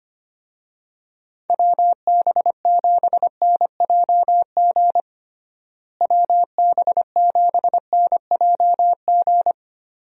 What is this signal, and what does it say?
Morse code, heard as a keyed tone, about 25 wpm, 700 Hz. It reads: WB7NJG WB7NJG
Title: USA Call Signs - 6 Characters - Repeated 3x - 25wpm